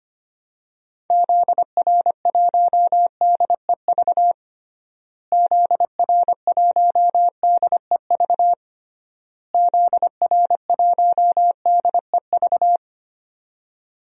Morse code, heard as a keyed tone, about 25 wpm, 690 Hz. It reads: ZR1DEV ZR1DEV ZR1DEV
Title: DXCC-100 Call Signs - 6 Characters - Repeated 3x - 25wpm